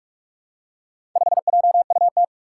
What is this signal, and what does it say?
Morse code, heard as a keyed tone, about 44 wpm, 700 Hz. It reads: 5JUT
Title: Four Challenging Letter-Number Combinations 45wpm